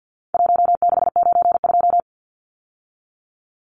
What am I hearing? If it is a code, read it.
Morse code, heard as a keyed tone, about 50 wpm, 700 Hz. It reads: 1692
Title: Numbers - 4-Digit 50wpm